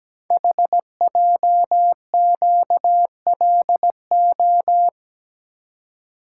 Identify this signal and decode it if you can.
Morse code, heard as a keyed tone, about 17 wpm, 700 Hz. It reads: HJQLO
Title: Code Groups - 5 Letters - 17wpm